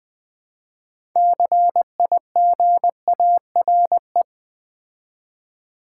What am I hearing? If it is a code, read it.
Morse code, heard as a keyed tone, about 20 wpm, 700 Hz. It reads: CIGARE
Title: Build A Word - 3 Letters At A Time - 9-Letter Words - 20wpm